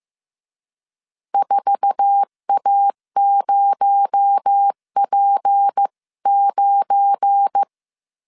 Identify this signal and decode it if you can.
Morse code, heard as a keyed tone, about 15 wpm, 800 Hz. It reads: 4A0P9
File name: CW / DET / 1 9 1500C S B 14wpm 3302